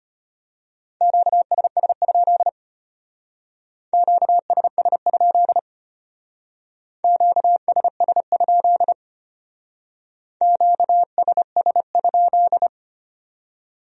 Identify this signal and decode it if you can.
Morse code, heard as a keyed tone, about 32 wpm, 700 Hz. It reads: QSS? QSS? QSS? QSS?